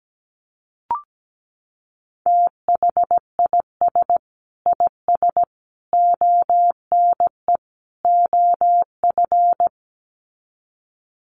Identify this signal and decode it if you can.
Morse code, heard as a keyed tone, about 17 wpm, 700 Hz. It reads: THIS IS ONE OF
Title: Sets of 4 Words 17wpm